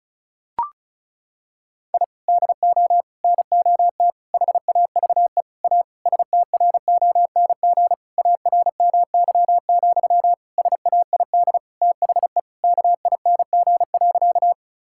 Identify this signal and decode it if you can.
Morse code, heard as a keyed tone, about 36 wpm, 700 Hz. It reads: I DO NOT HAVE A STRONG ARMY, SAID THE KING.
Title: Sentences from Top 1000 Words - Review Part 5 - 35wpm